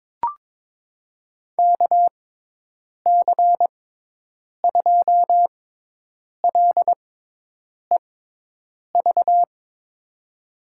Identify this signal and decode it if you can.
Morse code, heard as a keyed tone, about 22 wpm, spaced as for 8 wpm, 700 Hz. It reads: KC2LEV
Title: USA Call Signs - 6 Characters - 6x Character Spacing - 22wpm